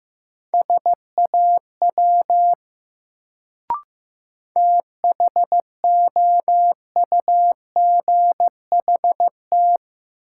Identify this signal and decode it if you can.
Morse code, heard as a keyed tone, about 15 wpm, 700 Hz. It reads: SAW THOUGHT
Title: Sentences from Top 300 Words - New Word Drill - 15wpm